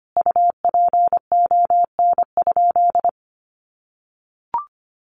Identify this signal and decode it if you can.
Morse code, heard as a keyed tone, about 25 wpm, 700 Hz. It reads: UPON?
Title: Sentences from Top 1100 Words 25wpm